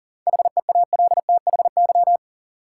Morse code, heard as a keyed tone, about 40 wpm, 700 Hz. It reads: HEALTHY
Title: Sentences from Top 1000 Words - Review Part 5 - New Word Drill - 40wpm